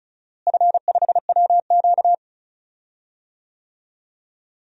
Morse code, heard as a keyed tone, about 34 wpm, 700 Hz. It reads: F5WQ
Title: DXCC-100 Call Signs - 4 Characters - Repeated 2x - 35wpm